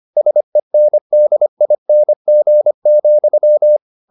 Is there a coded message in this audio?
Yes: Morse code from 0.2 to 3.8 s, heard as a keyed tone, about 25 wpm, 600 Hz. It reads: SENDING,